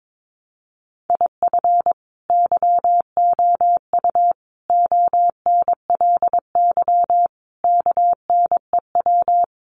Morse code, heard as a keyed tone, about 22 wpm, 700 Hz. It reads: IF YOU ONLY KNEW
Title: Sentences from Top 1500 Words 22wpm